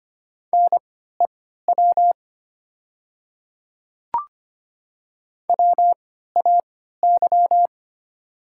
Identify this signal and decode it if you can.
Morse code, heard as a keyed tone, about 25 wpm, spaced as for 14 wpm, 700 Hz. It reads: NEW WAY
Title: 3x Character Spacing - Top 300 Words - 25wpm